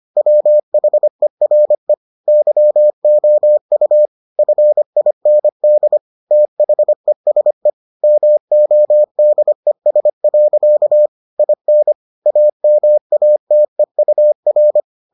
Morse code, heard as a keyed tone, about 25 wpm, 600 Hz. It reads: WHERE YOU FIND THESE MODES. IN AMATEUR